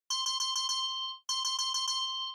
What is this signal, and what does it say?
There is no code.